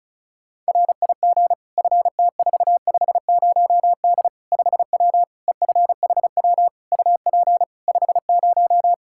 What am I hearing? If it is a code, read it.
Morse code, heard as a keyed tone, about 34 wpm, 700 Hz. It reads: RIG FT450D 5W EFHW UP 50
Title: Practice QSO - Exchange 2 - Info - 35wpm